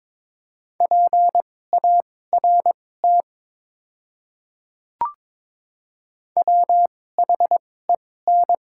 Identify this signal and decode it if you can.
Morse code, heard as a keyed tone, about 23 wpm, spaced as for 16 wpm, 700 Hz. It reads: PART WHEN
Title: 2x Character Spacing - Top 500 Words - 22wpm